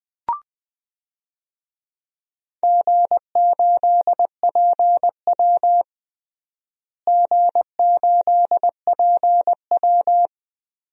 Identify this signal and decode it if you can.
Morse code, heard as a keyed tone, about 20 wpm, 700 Hz. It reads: G8PW G8PW
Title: DXCC-100 Call Signs - 4 Characters - Repeated 2x - 20wpm